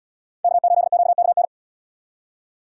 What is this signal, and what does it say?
Morse code, heard as a keyed tone, about 74 wpm, 700 Hz. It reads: K8QKN